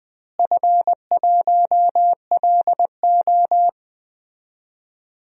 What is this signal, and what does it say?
Morse code, heard as a keyed tone, about 20 wpm, 700 Hz. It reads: F1LO
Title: DXCC-100 Call Signs - 4 Characters - Repeated 2x - 20wpm